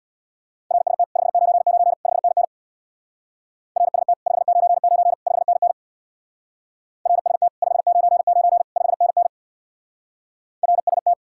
Morse code, heard as a keyed tone, about 64 wpm, 690 Hz. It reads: RST 599 5NN RST 599 5NN RST 599 5NN RST